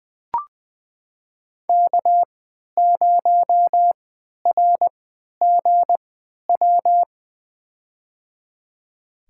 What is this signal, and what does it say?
Morse code, heard as a keyed tone, about 20 wpm, spaced as for 11 wpm, 700 Hz. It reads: K0RGW
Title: USA Call Signs - 5 Characters - 3x Character Spacing - 20wpm